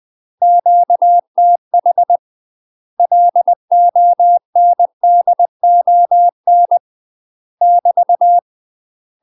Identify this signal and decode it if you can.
Morse code, heard as a keyed tone, about 20 wpm, 700 Hz. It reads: QTH LONDON <BT>
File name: UK MASTER QSOs TRAINING / 20WPM / UK MASTER QSOs 0003